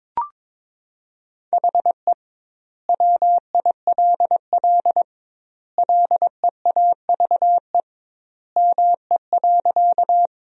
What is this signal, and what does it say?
Morse code, heard as a keyed tone, about 22 wpm, 700 Hz. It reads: HE WILL LEAVE ME.
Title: 2x Word-Spacing - Sentences from Top 400 Words - 22wpm